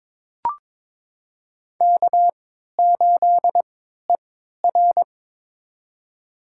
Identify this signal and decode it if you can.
Morse code, heard as a keyed tone, about 22 wpm, spaced as for 12 wpm, 700 Hz. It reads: K8ER
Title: USA Call Signs - 4 Characters - 3x Character Spacing - 22wpm